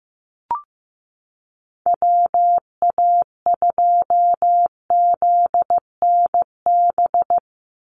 Morse code, heard as a keyed tone, about 15 wpm, 700 Hz. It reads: WA2ZNB
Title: USA Call Signs 15wpm